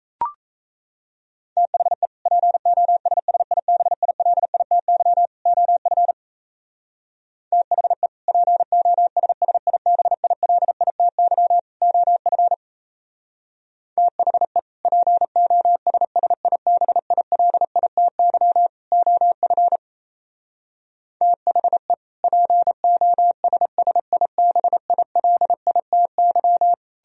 Morse code, heard as a keyed tone, about 34 wpm, 700 Hz. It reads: THE POSSIBILITY OF THE POSSIBILITY OF THE POSSIBILITY OF THE POSSIBILITY
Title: Speed-Racing - Sets of 3 Words 28wpm